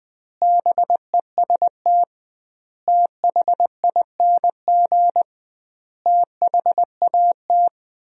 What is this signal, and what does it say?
Morse code, heard as a keyed tone, about 20 wpm, 700 Hz. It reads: BEST THING THAT
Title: 2x Word-Spacing - Sentences from Top 400 Words - 20wpm